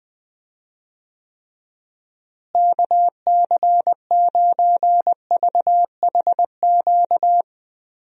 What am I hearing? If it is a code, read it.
Morse code, heard as a keyed tone, about 20 wpm, 700 Hz. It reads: KC9VHQ
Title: USA Call Signs - 6 Characters - Repeated 2x - 20wpm